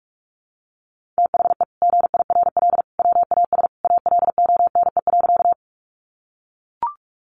Morse code, heard as a keyed tone, about 44 wpm, 700 Hz. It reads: THE GIRL WAS ALONE.
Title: Sentences from Top 1000 Words 45wpm